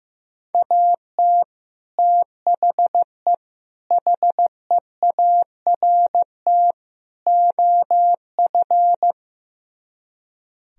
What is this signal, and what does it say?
Morse code, heard as a keyed tone, about 15 wpm, 700 Hz. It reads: AT THE HEART OF
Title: Speed-Racing - Sets of 4 Words 15wpm